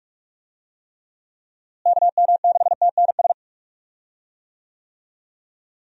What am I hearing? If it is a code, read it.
Morse code, heard as a keyed tone, about 41 wpm, 700 Hz. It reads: KM6TNS
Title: USA Call Signs - 6 Characters - Repeated 2x - 45wpm